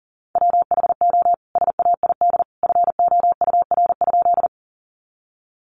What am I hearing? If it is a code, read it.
Morse code, heard as a keyed tone, about 40 wpm, 700 Hz. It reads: WHO SAID FOUR?
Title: Sentences from Top 300 Words 40wpm